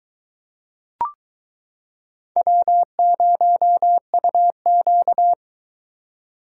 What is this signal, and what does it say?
Morse code, heard as a keyed tone, about 22 wpm, 700 Hz. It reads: W0UQ